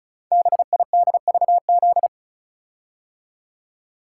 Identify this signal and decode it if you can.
Morse code, heard as a keyed tone, about 36 wpm, 700 Hz. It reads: BIDVZ
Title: Code Groups - 5 Letter:Numbers - 35wpm